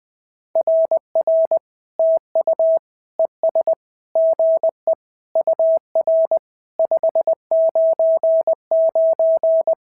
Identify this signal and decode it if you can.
Morse code, heard as a keyed tone, about 18 wpm, 650 Hz. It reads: RR TU ES GE UR 599
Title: SOTA QSO Practice 20wpm